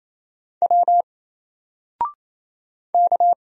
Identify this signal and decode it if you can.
Morse code, heard as a keyed tone, about 27 wpm, 700 Hz. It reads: W K
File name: Single-Letter-Number-28wpm